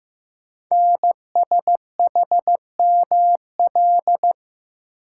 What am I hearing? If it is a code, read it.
Morse code, heard as a keyed tone, about 15 wpm, 700 Hz. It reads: NSHML
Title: Code Groups - 5 Letters - 15wpm